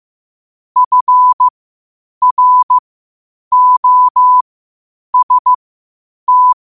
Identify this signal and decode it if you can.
Morse code, heard as a keyed tone, about 15 wpm, spaced as for 8 wpm, 1000 Hz. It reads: FROST
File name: qotdfiles / 8wpm